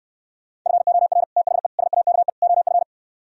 Morse code, heard as a keyed tone, about 67 wpm, 700 Hz. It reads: FOR THE SAKE OF